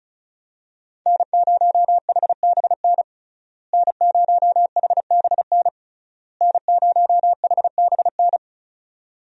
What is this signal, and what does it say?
Morse code, heard as a keyed tone, about 36 wpm, 700 Hz. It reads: N0HBN N0HBN N0HBN
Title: USA Call Signs - 5 Characters - Repeated 3x - 35wpm